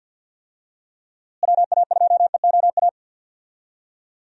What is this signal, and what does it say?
Morse code, heard as a keyed tone, about 52 wpm, 700 Hz. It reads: WA2EOA
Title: USA Call Signs - 6 Characters - Repeated 3x - 50wpm